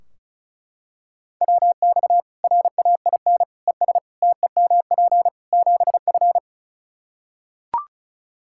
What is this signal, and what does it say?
Morse code, heard as a keyed tone, about 34 wpm, 700 Hz. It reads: WX RAIN ES TEMP 7F
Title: Practice QSO - Exchange 2 - Info - 35wpm